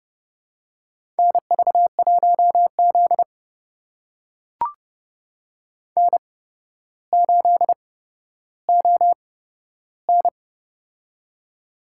Morse code, heard as a keyed tone, about 32 wpm, spaced as for 13 wpm, 700 Hz. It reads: NV1Z N8ON
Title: USA Call Signs - 4 Characters - 8x Character Spacing - 30wpm